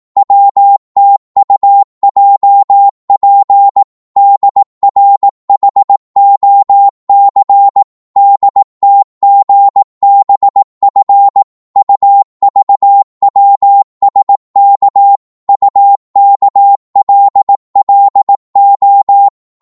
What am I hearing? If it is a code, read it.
Morse code, heard as a keyed tone, about 18 wpm, 800 Hz. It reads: WTUJP DRHOC DTGBF UVWSK UKLLO
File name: Morse / ltr-18wpm